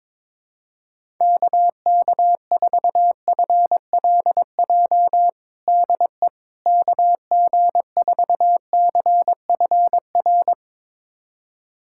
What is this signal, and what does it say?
Morse code, heard as a keyed tone, about 22 wpm, 700 Hz. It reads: KK4FLJ DE KG4CFR
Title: USA Callsign DE Callsign - 6 Characters - 22wpm